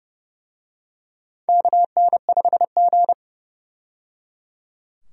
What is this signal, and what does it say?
Morse code, heard as a keyed tone, about 30 wpm, 700 Hz. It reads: KN5G
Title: USA Call Signs - 4 Characters - 30wpm